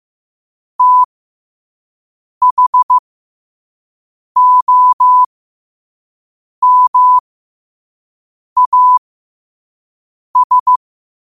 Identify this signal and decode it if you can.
Morse code, heard as a keyed tone, about 15 wpm, spaced as for 5 wpm, 1000 Hz. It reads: THOMAS
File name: qotdfiles / 5wpm